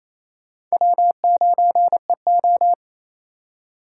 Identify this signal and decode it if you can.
Morse code, heard as a keyed tone, about 37 wpm, 700 Hz. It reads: W9EO